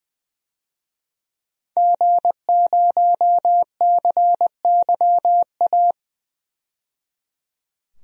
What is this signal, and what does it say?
Morse code, heard as a keyed tone, about 20 wpm, 700 Hz. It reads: G0CYA
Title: DXCC-100 Call Signs - 5 Characters - 20wpm